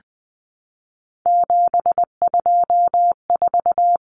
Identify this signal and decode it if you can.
Morse code, heard as a keyed tone, about 20 wpm, 700 Hz. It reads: 724
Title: Numbers - 3-Digit 20wpm